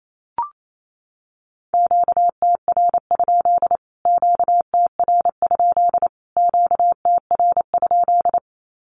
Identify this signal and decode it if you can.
Morse code, heard as a keyed tone, about 28 wpm, 700 Hz. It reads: QTR? QTR? QTR?
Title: Q-Codes Repeated 3x 28wpm